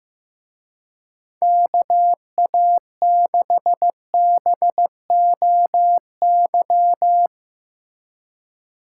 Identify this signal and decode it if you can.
Morse code, heard as a keyed tone, about 16 wpm, 700 Hz. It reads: KA6BOY
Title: Speed Racing - USA Call Signs 15wpm